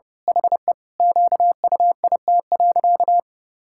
Morse code, heard as a keyed tone, about 28 wpm, 700 Hz. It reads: HE QUIT.
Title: Pangram Challenge v2 - 30wpm